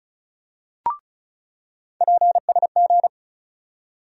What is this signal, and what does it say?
Morse code, heard as a keyed tone, about 34 wpm, 700 Hz. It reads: PSG
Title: USA Call Sign Postfixes - 3 Letters 35wpm